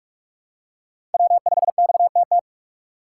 Morse code, heard as a keyed tone, about 42 wpm, 700 Hz. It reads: W5XTT
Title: USA Call Signs - 5 Characters - 2x Character Spacing - 45wpm